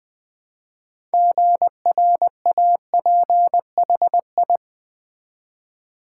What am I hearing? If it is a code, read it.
Morse code, heard as a keyed tone, about 20 wpm, 700 Hz. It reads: GRAPHI
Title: Build A Word - 3 Letters At A Time - 8-Letter Words - 20wpm